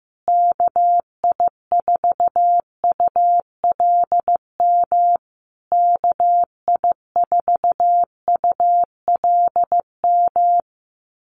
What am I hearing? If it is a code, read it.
Morse code, heard as a keyed tone, about 14 wpm, 700 Hz. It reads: KI4ULM KI4ULM
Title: USA Call Signs Repeated 3x 15wpm